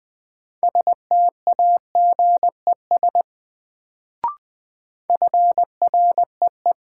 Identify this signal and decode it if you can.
Morse code, heard as a keyed tone, about 20 wpm, 700 Hz. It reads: STAGES FREE
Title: Sets of 5 Words Encore - New Word Drill - 20wpm